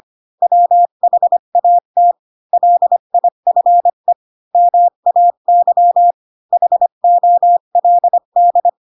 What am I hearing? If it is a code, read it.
Morse code, heard as a keyed tone, about 25 wpm, 700 Hz. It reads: WHAT LIFE MAY HOLD
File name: Pinocchio / Pinocchio ch7 20 25 ik1hge